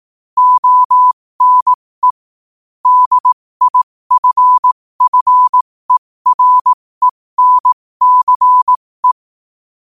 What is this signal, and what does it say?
Morse code, heard as a keyed tone, about 18 wpm, spaced as for 15 wpm, 1000 Hz. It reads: ONE DIFFERENCE
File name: qotdfiles / 15wpm